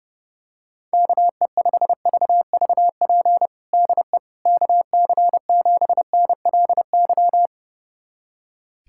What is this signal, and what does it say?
Morse code, heard as a keyed tone, about 32 wpm, 700 Hz. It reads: KE5VVP DE KC7NLY
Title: USA Callsign DE Callsign 30wpm